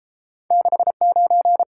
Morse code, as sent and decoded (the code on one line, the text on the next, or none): -.... ----.
69